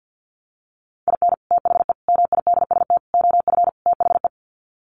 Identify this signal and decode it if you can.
Morse code, heard as a keyed tone, about 50 wpm, 700 Hz. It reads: IN THE MIDST OF THE
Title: Sets of 5 Words 50wpm